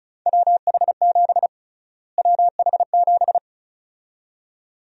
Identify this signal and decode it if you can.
Morse code, heard as a keyed tone, about 36 wpm, 700 Hz. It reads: WH7 WH7
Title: USA Call Sign Prefixes - Repeated 2x - 35wpm